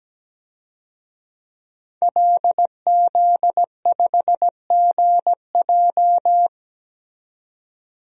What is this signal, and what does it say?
Morse code, heard as a keyed tone, about 17 wpm, 700 Hz. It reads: LZ5GJ
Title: DXCC-100 Call Signs - 5 Characters - 17wpm